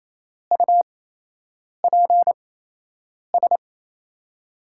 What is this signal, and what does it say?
Morse code, heard as a keyed tone, about 28 wpm, spaced as for 8 wpm, 700 Hz. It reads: UPS
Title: USA Call Signs - 5 Characters - 8x Character Spacing - 28wpm